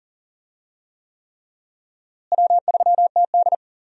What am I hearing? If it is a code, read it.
Morse code, heard as a keyed tone, about 40 wpm, 700 Hz. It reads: W3TD
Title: USA Call Signs - 4 Characters - Repeated 3x - 40wpm